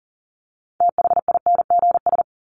Morse code, heard as a keyed tone, about 40 wpm, 700 Hz. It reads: THINGS
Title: ES:S Suffix - Top 100 Words with Base Words 40wpm